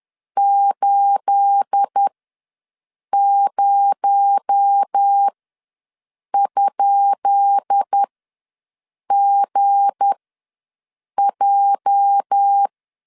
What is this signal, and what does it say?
Morse code, heard as a keyed tone, about 10 wpm, spaced as for 6 wpm, 800 Hz. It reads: 80?GJ
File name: LEZIONI / 10 SIGNS / 10B INT ws 10wpm 4134